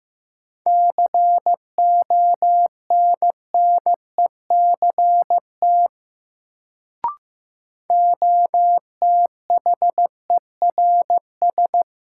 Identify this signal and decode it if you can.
Morse code, heard as a keyed tone, about 15 wpm, 700 Hz. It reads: CONNECT OTHERS
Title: Sentences from Top 1500 Words - New Word Drill Alt - 15wpm